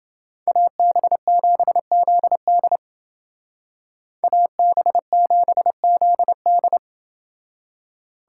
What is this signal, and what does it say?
Morse code, heard as a keyed tone, about 29 wpm, 700 Hz. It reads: AB7ZD AB7ZD